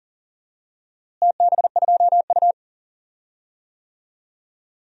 Morse code, heard as a keyed tone, about 40 wpm, 700 Hz. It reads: TB2U
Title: DXCC-100 Call Signs - 4 Characters - Repeated 2x - 40wpm